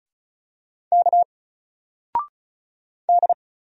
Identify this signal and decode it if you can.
Morse code, heard as a keyed tone, about 35 wpm, 700 Hz. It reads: K D
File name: Single-Letter-Number-35wpm